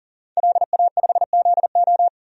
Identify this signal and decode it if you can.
Morse code, heard as a keyed tone, about 40 wpm, 700 Hz. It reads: LA5ZO
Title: DXCC-100 Call Signs - 5 Characters - 40wpm